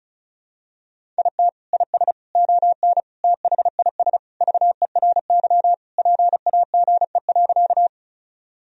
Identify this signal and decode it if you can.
Morse code, heard as a keyed tone, about 36 wpm, 700 Hz. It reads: IT IS ON THIS VERY PAGE.
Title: Sentences from Top 1000 Words - Review Part 1 - 35wpm